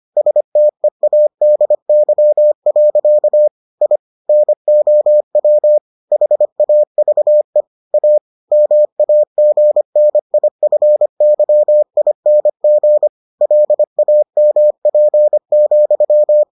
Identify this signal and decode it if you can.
Morse code, heard as a keyed tone, about 25 wpm, 600 Hz. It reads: STEADY. I NOW HAVE A MAGNIFYING LAMP,